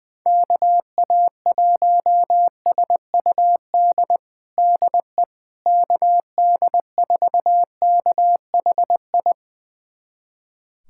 Morse code, heard as a keyed tone, about 20 wpm, 700 Hz. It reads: KA1SUD DE KD4KHI
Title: USA Callsign DE Callsign - 6 Characters - 20wpm